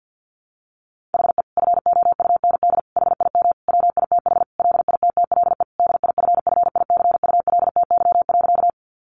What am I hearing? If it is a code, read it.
Morse code, heard as a keyed tone, about 50 wpm, 700 Hz. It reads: HE FOUND HIM WITH LITTLE DIFFICULTY.
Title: Sentences from Top 1500 Words 50wpm